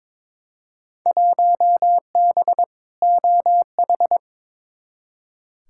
Morse code, heard as a keyed tone, about 22 wpm, 700 Hz. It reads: 1B OH
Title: Field Day Exchanges - 22wpm